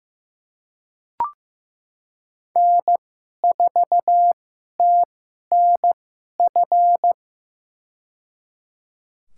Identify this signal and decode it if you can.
Morse code, heard as a keyed tone, about 15 wpm, spaced as for 11 wpm, 700 Hz. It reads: N4TNF
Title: USA Call Signs - 5 Characters - 2x Character Spacing - 15wpm